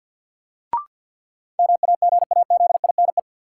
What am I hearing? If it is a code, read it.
Morse code, heard as a keyed tone, about 52 wpm, 700 Hz. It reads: MAGAZINE
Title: Sentences from Top 1000 Words - Review Part 7 - New Word Drill - 50wpm